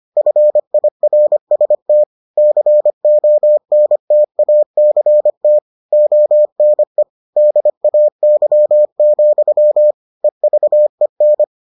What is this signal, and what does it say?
Morse code, heard as a keyed tone, about 25 wpm, 600 Hz. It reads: FIRST CONTACT ONE DAY, EVEN